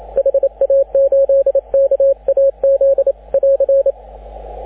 Morse code, heard as a keyed tone, about 28 wpm, 550 Hz, with radio noise. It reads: HA8KAZ <AR>
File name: AUDIO2019 / HG6N CW SSB MIX